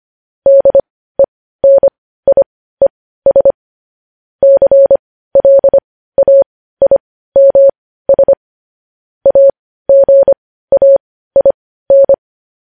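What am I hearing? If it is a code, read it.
Morse code, heard as a keyed tone, about 25 wpm, spaced as for 15 wpm, 550 Hz. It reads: DENIES CLAIMS AGAIN
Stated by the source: Morse news headlines practice at 15 wpm.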